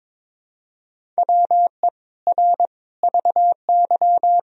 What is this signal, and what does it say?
Morse code, heard as a keyed tone, about 22 wpm, 700 Hz. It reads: WE R VY
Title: Sentences from Top 300 Words and Abbreviations 22wpm